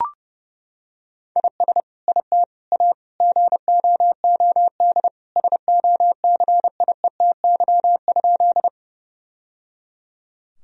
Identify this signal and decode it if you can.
Morse code, heard as a keyed tone, about 30 wpm, 700 Hz. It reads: IS IT A GOOD SOCIETY?
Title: Sentences from Top 1000 Words - Review Part 4 - 30wpm